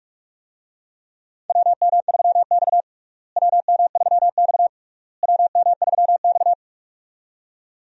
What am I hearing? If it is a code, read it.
Morse code, heard as a keyed tone, about 41 wpm, 700 Hz. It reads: WM3X WM3X WM3X
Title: USA Call Signs - 4 Characters - Repeated 3x - 45wpm